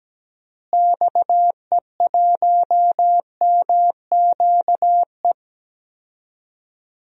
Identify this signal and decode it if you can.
Morse code, heard as a keyed tone, about 17 wpm, 700 Hz. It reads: XE1MQE
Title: DXCC-100 Call Signs - 6 Characters - Repeated 3x - 17wpm